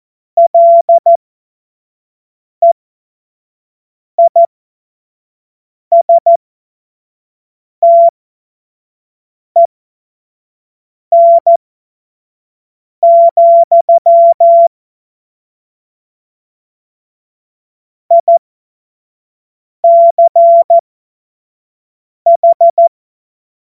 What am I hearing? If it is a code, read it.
Morse code, heard as a keyed tone, about 14 wpm, spaced as for 5 wpm, 680 Hz. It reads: LEISTEN, ICH
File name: Morsekurs / Witze / Witz 12 März 2024 25 F70